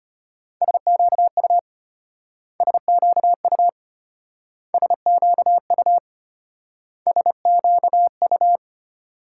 Morse code, heard as a keyed tone, about 32 wpm, 700 Hz. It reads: SQU SQU SQU SQU